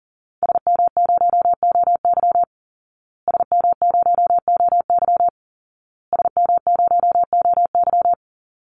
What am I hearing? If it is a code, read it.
Morse code, heard as a keyed tone, about 40 wpm, 700 Hz. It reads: SM0OY SM0OY SM0OY
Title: Call Signs Repeated 2x - 40wpm